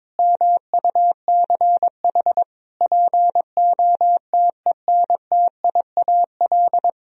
Morse code, heard as a keyed tone, about 21 wpm, 700 Hz. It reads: MUCH POTENTIAL
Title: Sentences from Top 1000 Words - Review Part 4 - 22wpm